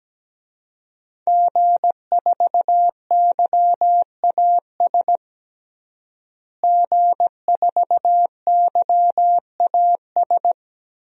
Morse code, heard as a keyed tone, about 17 wpm, 700 Hz. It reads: G4YAS G4YAS
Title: DXCC-100 Call Signs - 5 Characters - Repeated 2x - 17wpm